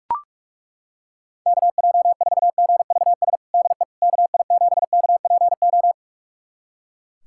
Morse code, heard as a keyed tone, about 41 wpm, 700 Hz. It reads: KJ4GVS DE KI7KPO
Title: USA Callsign DE Callsign - 6 Characters - 45wpm